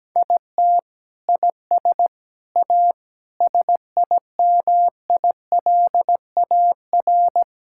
Morse code, heard as a keyed tone, about 17 wpm, 700 Hz. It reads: IT IS A SIMILAR
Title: Sentences from Top 1000 Words - Review Part 5 - 17wpm